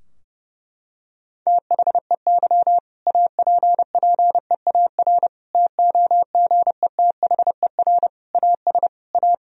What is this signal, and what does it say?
Morse code, heard as a keyed tone, about 30 wpm, 700 Hz. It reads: THEY APPEAR TOGETHER AS A
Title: Sentences from Top 1000 Words - Review Part 2 - 30wpm